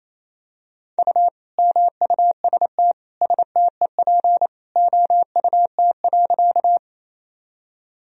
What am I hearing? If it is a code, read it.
Morse code, heard as a keyed tone, about 28 wpm, 700 Hz. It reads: U MUST STEP OUT.
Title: Sentences from Top 400 Words and Abbreviations 28wpm